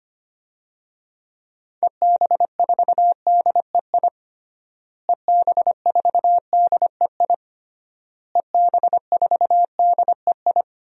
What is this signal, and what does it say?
Morse code, heard as a keyed tone, about 25 wpm, 700 Hz. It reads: EB4DEI EB4DEI EB4DEI
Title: DXCC-100 Call Signs - 6 Characters - Repeated 3x - 25wpm